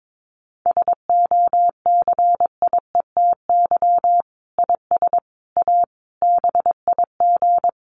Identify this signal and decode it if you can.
Morse code, heard as a keyed tone, about 22 wpm, 700 Hz. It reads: SOCIETY IS A BIG
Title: Sentences from Top 1500 Words 22wpm